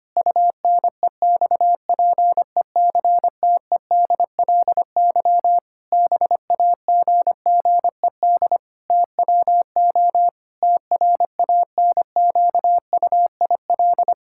Morse code, heard as a keyed tone, about 25 wpm, 700 Hz. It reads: UNEXPECTEDLY BAGGED TWO TRANQUIL
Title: Pangram Challenge v2 - 25wpm